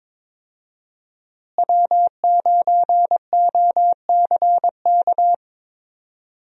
Morse code, heard as a keyed tone, about 21 wpm, 700 Hz. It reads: W9OCK
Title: USA Call Signs - 5 Characters - Repeated 2x - 22wpm